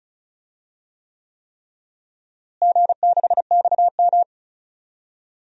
Morse code, heard as a keyed tone, about 34 wpm, 700 Hz. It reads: G6XM
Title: DXCC-100 Call Signs - 4 Characters - Repeated 3x - 35wpm